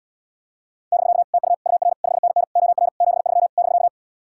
Morse code, heard as a keyed tone, about 79 wpm, 700 Hz. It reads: <BK> TU UR 5NN OR 73 <BK>